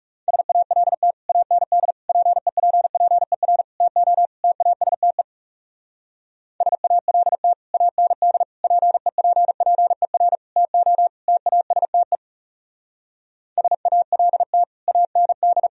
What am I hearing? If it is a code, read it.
Morse code, heard as a keyed tone, about 35 wpm, 700 Hz. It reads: SALT AND PEPPER TO TASTE SALT AND PEPPER TO TASTE SALT AND
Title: Speed-Racing - Sets of 5 Words 30wpm